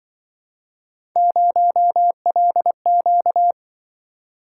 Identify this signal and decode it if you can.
Morse code, heard as a keyed tone, about 26 wpm, 700 Hz. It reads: 0LQ